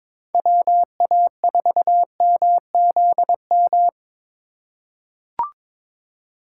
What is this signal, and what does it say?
Morse code, heard as a keyed tone, about 22 wpm, 700 Hz. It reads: WA4MZM
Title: USA Call Signs - 6 Characters - 4x Character Spacing - 22wpm